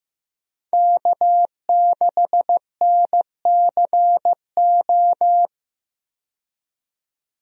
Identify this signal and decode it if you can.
Morse code, heard as a keyed tone, about 15 wpm, 700 Hz. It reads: K6NCO
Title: USA Call Signs - 5 Characters - 15wpm